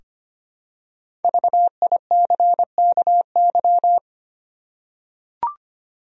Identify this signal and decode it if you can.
Morse code, heard as a keyed tone, about 25 wpm, 700 Hz. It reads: VICKY
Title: USA Common Names - Max 5 Letters - 25wpm